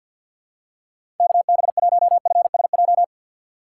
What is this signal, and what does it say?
Morse code, heard as a keyed tone, about 51 wpm, 700 Hz. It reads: KB1FSJ
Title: USA Call Signs - 2x Character Spacing - 50wpm